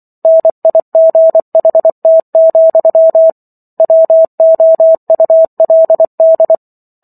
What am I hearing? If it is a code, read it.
Morse code, heard as a keyed tone, about 24 wpm, 650 Hz. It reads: NIGHT, WOULD
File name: Chapter 15 - 24 WPM